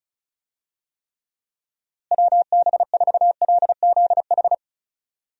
Morse code, heard as a keyed tone, about 34 wpm, 700 Hz. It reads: WB4LZH
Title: USA Call Signs - 6 Characters - Repeated 3x - 35wpm